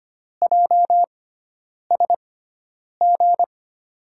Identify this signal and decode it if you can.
Morse code, heard as a keyed tone, about 25 wpm, spaced as for 9 wpm, 700 Hz. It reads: JSG
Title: USA Call Signs - 5 Characters - 6x Character Spacing - 25wpm